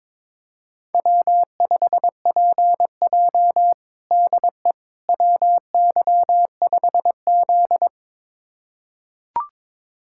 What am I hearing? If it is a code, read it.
Morse code, heard as a keyed tone, about 22 wpm, 700 Hz. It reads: W5PJ DE WY5Z
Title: USA Callsign DE Callsign - 4 Characters - 22wpm